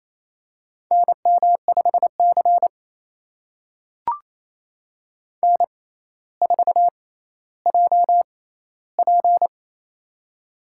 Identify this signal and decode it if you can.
Morse code, heard as a keyed tone, about 28 wpm, spaced as for 14 wpm, 700 Hz. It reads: NM5C N4JP
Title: USA Call Signs - 4 Characters - 6x Character Spacing - 28wpm